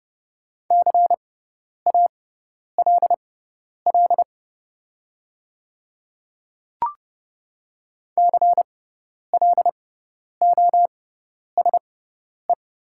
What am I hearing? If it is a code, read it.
Morse code, heard as a keyed tone, about 30 wpm, spaced as for 10 wpm, 700 Hz. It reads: CALL CLOSE
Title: 6x Character Spacing - Top 300 Words - 30wpm